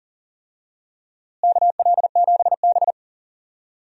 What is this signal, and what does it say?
Morse code, heard as a keyed tone, about 40 wpm, 700 Hz. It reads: KL7B
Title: DXCC-100 Call Signs - 4 Characters - Repeated 3x - 40wpm